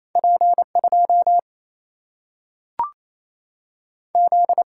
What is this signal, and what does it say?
Morse code, heard as a keyed tone, about 28 wpm, 700 Hz. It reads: P2 Z
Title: Single-Letter Single-Number 28wpm